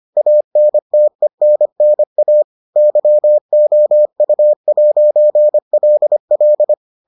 Morse code, heard as a keyed tone, about 25 wpm, 600 Hz. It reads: ANTENNA YOU'LL